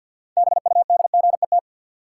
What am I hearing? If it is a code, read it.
Morse code, heard as a keyed tone, about 51 wpm, 700 Hz. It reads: BUDGET
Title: Sentences from Top 1200 Words - New Word Drill Alt - 50wpm